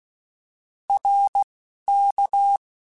Morse code, heard as a keyed tone, about 16 wpm, 800 Hz. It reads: R K
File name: Contrasts / Contrasts-mp3s / 2LtrContrasts @10